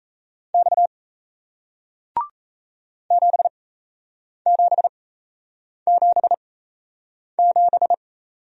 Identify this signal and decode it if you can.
Morse code, heard as a keyed tone, about 38 wpm, spaced as for 8 wpm, 700 Hz. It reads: X 7777